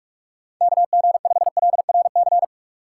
Morse code, heard as a keyed tone, about 46 wpm, 700 Hz. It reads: KG5LRC
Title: USA Call Signs - 6 Characters - 6x Character Spacing - 45wpm